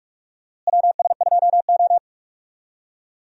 Morse code, heard as a keyed tone, about 45 wpm, 700 Hz. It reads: WS2O
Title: USA Call Signs - 4 Characters - Repeated 3x - 45wpm